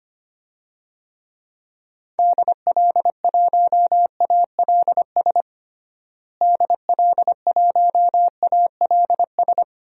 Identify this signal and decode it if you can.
Morse code, heard as a keyed tone, about 25 wpm, 700 Hz. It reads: DL1ALS DL1ALS
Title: DXCC-100 Call Signs - 6 Characters - Repeated 3x - 25wpm